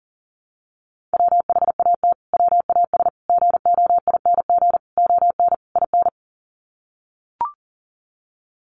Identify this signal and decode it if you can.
Morse code, heard as a keyed tone, about 40 wpm, 700 Hz. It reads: WHAT WAS GOING ON IN
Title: Sets of 5 Words Encore 40wpm